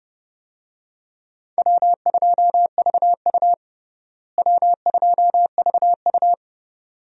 Morse code, heard as a keyed tone, about 30 wpm, 700 Hz. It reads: W2VU W2VU
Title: USA Call Signs - 4 Characters - Repeated 2x - 30wpm